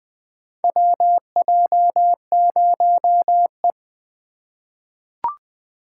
Morse code, heard as a keyed tone, about 20 wpm, 700 Hz. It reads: WJ0E
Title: USA Call Signs - 4 Characters - 8x Character Spacing - 20wpm